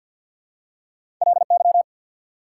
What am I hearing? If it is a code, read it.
Morse code, heard as a keyed tone, about 48 wpm, 690 Hz. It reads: LY
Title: Prefixes and Suffixes 50wpm